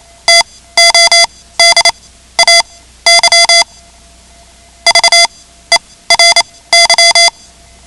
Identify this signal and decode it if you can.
Morse code, heard as a keyed tone, about 28 wpm, spaced as for 16 wpm, 2050 Hz. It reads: TODAY VERY
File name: ham radio / CodeQuick / Code Quick 10